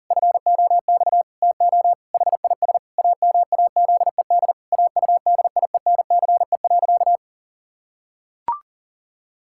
Morse code, heard as a keyed tone, about 37 wpm, 700 Hz. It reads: FOX TO HIS AMAZED AUDIENCE.
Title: Pangram Challenge v2 - 40wpm